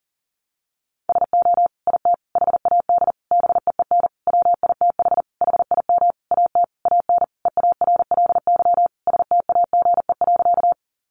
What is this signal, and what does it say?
Morse code, heard as a keyed tone, about 40 wpm, 700 Hz. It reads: SO IT HAD BEEN WITH HIM AT AN EARLY STAGE.
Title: Sentences from Top 1200 Words 40wpm